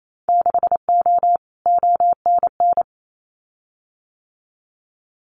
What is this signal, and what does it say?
Morse code, heard as a keyed tone, about 28 wpm, 700 Hz. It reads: 6O ONN
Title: Winter Field Day Exchanges Encore 28wpm